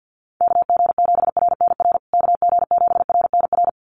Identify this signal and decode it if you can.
Morse code, heard as a keyed tone, about 48 wpm, 700 Hz. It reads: KG7RNR KG7RNR
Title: USA Call Signs Repeated 2x 50wpm